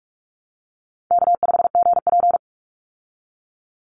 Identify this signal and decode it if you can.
Morse code, heard as a keyed tone, about 44 wpm, 700 Hz. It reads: K5GP
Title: Easy Call Signs Repeated 2x - 45wpm